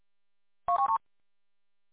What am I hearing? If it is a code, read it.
There is no code.